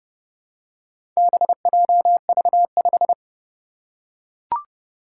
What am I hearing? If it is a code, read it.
Morse code, heard as a keyed tone, about 29 wpm, 700 Hz. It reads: BJV5
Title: Four Challenging Letter-Number Combinations 30wpm